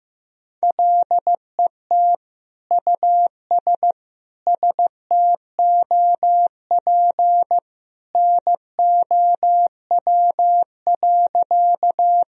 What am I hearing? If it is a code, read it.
Morse code, heard as a keyed tone, about 16 wpm, 700 Hz. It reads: LET US STOP NOW.
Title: Sentences from Top 1000 Words - Review Part 1 - 15wpm